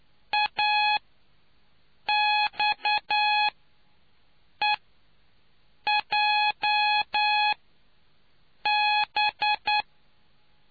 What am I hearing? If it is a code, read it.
Morse code, heard as a keyed tone, about 9 wpm, spaced as for 5 wpm, 840 Hz. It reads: AXEJB